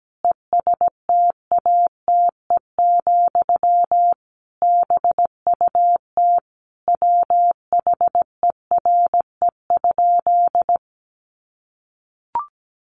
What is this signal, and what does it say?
Morse code, heard as a keyed tone, about 17 wpm, 700 Hz. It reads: ESTATE, BUT WHERE?
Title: Sentences from Top 1200 Words 17wpm